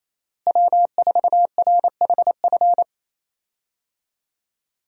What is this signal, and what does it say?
Morse code, heard as a keyed tone, about 28 wpm, 700 Hz. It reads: W4RHF
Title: USA Call Signs - 5 Characters - Repeated 3x - 28wpm